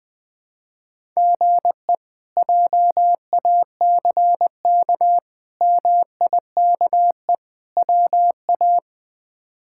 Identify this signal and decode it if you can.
Morse code, heard as a keyed tone, about 20 wpm, 700 Hz. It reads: GE JACK MIKE WA
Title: K1USN SST QSO Practice - 20wpm